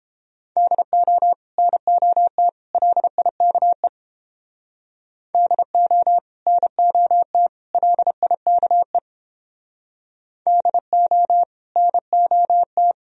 Morse code, heard as a keyed tone, about 29 wpm, 700 Hz. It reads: DO NOT LIKE DO NOT LIKE DO NOT